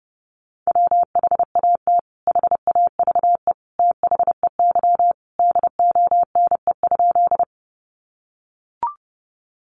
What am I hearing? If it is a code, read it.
Morse code, heard as a keyed tone, about 30 wpm, 700 Hz. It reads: WHAT HAVE THEY DONE?
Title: Sentences from Top 500 Words 30wpm